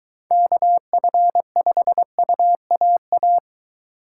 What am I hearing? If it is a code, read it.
Morse code, heard as a keyed tone, about 23 wpm, 700 Hz. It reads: KF5UAA